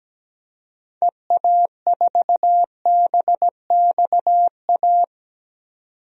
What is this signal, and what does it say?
Morse code, heard as a keyed tone, about 17 wpm, 700 Hz. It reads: EA4BXA
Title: DXCC-100 Call Signs - 6 Characters - 17wpm